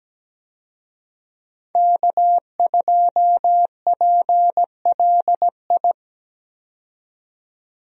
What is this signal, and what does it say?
Morse code, heard as a keyed tone, about 17 wpm, 700 Hz. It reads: K2PLI
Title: USA Call Signs - 5 Characters - 17wpm